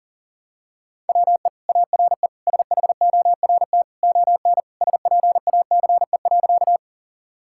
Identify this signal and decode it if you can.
Morse code, heard as a keyed tone, about 37 wpm, 700 Hz. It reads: WE ARE SHORT ON SPACE.
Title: Sentences from Top 1000 Words - Review Part 5 - 40wpm